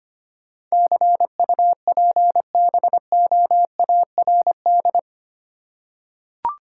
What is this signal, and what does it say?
Morse code, heard as a keyed tone, about 25 wpm, 700 Hz. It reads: CUPBOARD
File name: Eight Letter Words - 25wpm